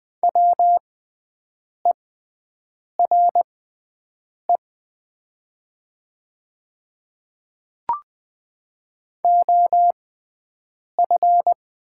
Morse code, heard as a keyed tone, about 20 wpm, spaced as for 7 wpm, 700 Hz. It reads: WERE OF
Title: 6x Character Spacing - Top 100 Words - 20wpm